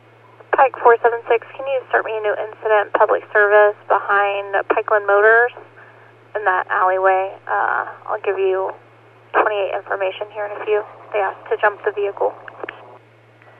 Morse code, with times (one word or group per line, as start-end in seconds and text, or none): none